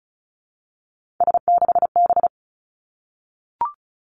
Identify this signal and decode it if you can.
Morse code, heard as a keyed tone, about 36 wpm, 700 Hz. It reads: S6B
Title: Three Challenging Letter-Number Combinations 35wpm